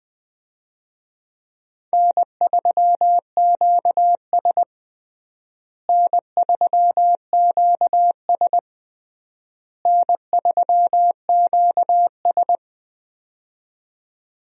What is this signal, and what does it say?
Morse code, heard as a keyed tone, about 21 wpm, 700 Hz. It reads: N3QS N3QS N3QS
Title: USA Call Signs - 4 Characters - Repeated 3x - 20wpm